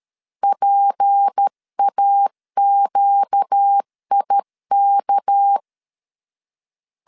Morse code, heard as a keyed tone, about 13 wpm, 800 Hz. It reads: PAQIK
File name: LEZIONI / 7 GF9Z / 7 G 12wpm 2055